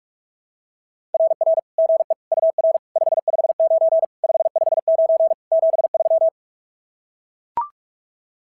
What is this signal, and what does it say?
Morse code, heard as a keyed tone, about 44 wpm, 690 Hz. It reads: RR GE UR 559 559 73
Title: SOTA QSO Practice 45wpm